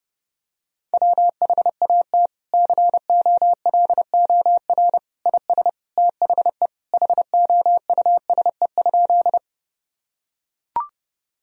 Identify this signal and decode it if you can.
Morse code, heard as a keyed tone, about 30 wpm, 700 Hz. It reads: WHAT COLOR IS THE HOUSE?
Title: Sentences from Top 1000 Words - Review Part 2 - 30wpm